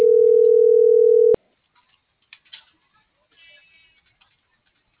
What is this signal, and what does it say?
Ringback tone (North America), 440 + 480 Hz.